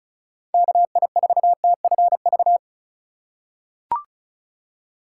Morse code, heard as a keyed tone, about 36 wpm, 700 Hz. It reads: KI4TFV
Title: USA Call Signs - 6 Characters - 8x Character Spacing - 35wpm